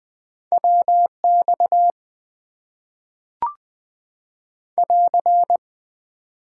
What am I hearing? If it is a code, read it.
Morse code, heard as a keyed tone, about 20 wpm, 700 Hz. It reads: WX <AR>
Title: Top 90 QSO Elements - 8x Character Spacing - 20wpm